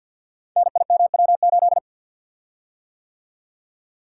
Morse code, heard as a keyed tone, about 45 wpm, 700 Hz. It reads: NIMW8
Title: Code Groups - 5 Letter:Numbers - 50wpm